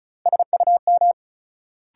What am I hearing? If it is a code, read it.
Morse code, heard as a keyed tone, about 34 wpm, 700 Hz. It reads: SUM
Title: Sentences from Top 1000 Words - Review Part 6 - New Word Drill - 35wpm